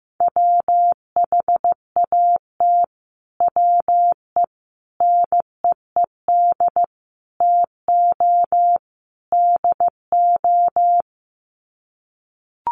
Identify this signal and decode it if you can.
Morse code, heard as a keyed tone, about 15 wpm, 700 Hz. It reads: WHAT WE NEED TO DO
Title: Sets of 5 Words Encore 15wpm